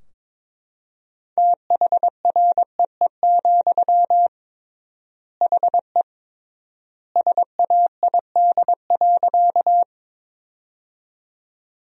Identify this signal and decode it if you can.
Morse code, heard as a keyed tone, about 21 wpm, 700 Hz. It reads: THREE, HE SAID.
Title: 3x Word-Spacing - Sentences from Top 200 Words - 22wpm